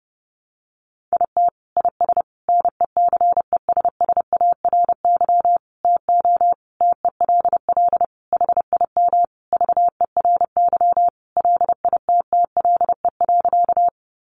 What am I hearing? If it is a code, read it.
Morse code, heard as a keyed tone, about 30 wpm, 700 Hz. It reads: IT IS NECESSARY TO TELL HIM VERY LITTLE.
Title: Sentences from Top 800 Words 30wpm